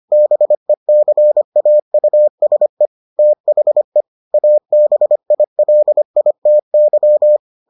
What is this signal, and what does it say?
Morse code, heard as a keyed tone, about 25 wpm, 600 Hz. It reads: BECAUSE THE ABILITY